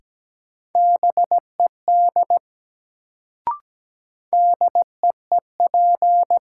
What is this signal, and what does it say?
Morse code, heard as a keyed tone, about 17 wpm, 700 Hz. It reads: BED DEEP
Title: Sentences from Top 500 Words - New Word Drill - 17wpm